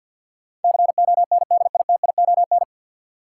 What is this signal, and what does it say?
Morse code, heard as a keyed tone, about 52 wpm, 700 Hz. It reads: CONDITION
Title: Sentences from Top 1000 Words - New Word Drill - 50wpm